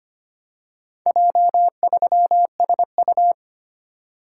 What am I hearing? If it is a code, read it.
Morse code, heard as a keyed tone, about 25 wpm, 700 Hz. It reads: J3SU
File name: practice / 25wpm / Four Letter-Number Combinations - 25wpm